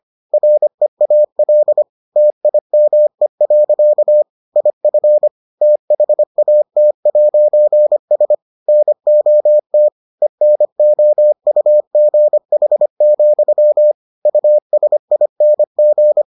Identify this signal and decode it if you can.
Morse code, heard as a keyed tone, about 25 wpm, 600 Hz. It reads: REAL TIME. IF THAT'S NOT ENOUGH, USING